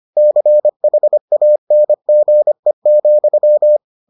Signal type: Morse code

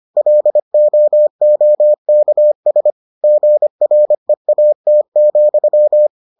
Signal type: Morse code

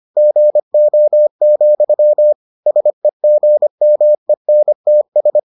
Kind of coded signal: Morse code

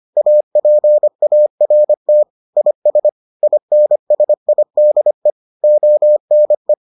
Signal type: Morse code

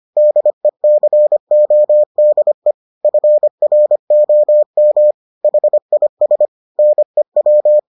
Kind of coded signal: Morse code

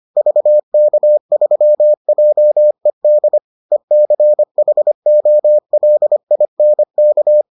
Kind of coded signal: Morse code

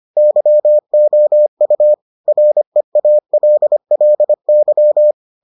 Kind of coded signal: Morse code

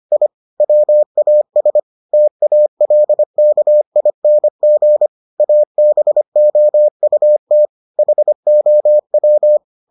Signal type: Morse code

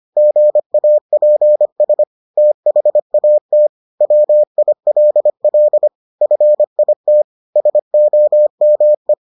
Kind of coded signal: Morse code

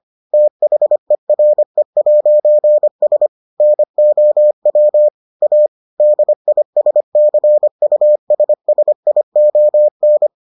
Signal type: Morse code

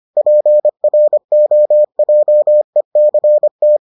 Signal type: Morse code

